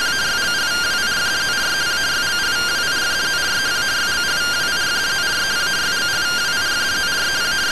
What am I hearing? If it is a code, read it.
Two-tone teleprinter signal, about 170 Hz shift.